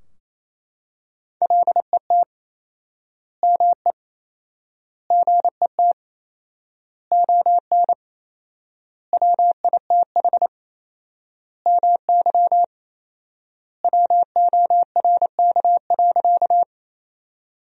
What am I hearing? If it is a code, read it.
Morse code, heard as a keyed tone, about 28 wpm, 700 Hz. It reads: LET ME GET ON WITH MY WORK.
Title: 4x Word-Spacing - Sentences from Top 300 Words - 28wpm